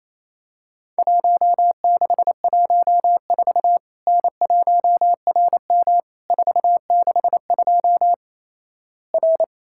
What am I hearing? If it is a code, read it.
Morse code, heard as a keyed tone, about 28 wpm, 700 Hz. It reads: 1614 N1RM 462 R
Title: Worked All Europe DX Contest - QTC Exchange - 28wpm